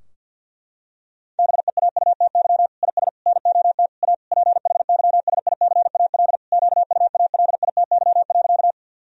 Morse code, heard as a keyed tone, about 52 wpm, 700 Hz. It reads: BEAUTY IS NOT A PHYSICAL QUALITY.
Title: Sentences from Top 1000 Words - Review Part 4 - 50wpm